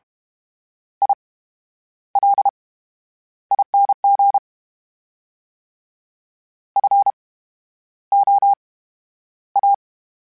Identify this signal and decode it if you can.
Morse code, heard as a keyed tone, about 32 wpm, spaced as for 11 wpm, 800 Hz. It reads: ILING FOA
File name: Special INGs / Special INGs-mp3s / VerySlowToFaster Ing @30